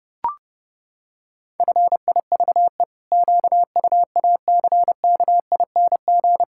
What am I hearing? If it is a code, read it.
Morse code, heard as a keyed tone, about 28 wpm, 700 Hz. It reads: FIVE QUACKING
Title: Pangram Challenge v2 - 30wpm